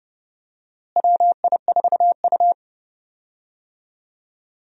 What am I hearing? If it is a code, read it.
Morse code, heard as a keyed tone, about 30 wpm, 700 Hz. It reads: WI4U
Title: USA Call Signs - 4 Characters - Repeated 3x - 30wpm